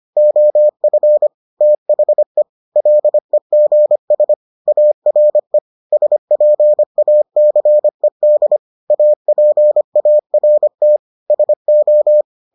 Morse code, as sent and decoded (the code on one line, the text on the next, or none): --- ..-. / - .... / .-.. --. ... / .- .-. / ... .--. .- -.-. -.. / .- .--. .- .-. - / ... ---
OF THE LEGS ARE SPACED APART SO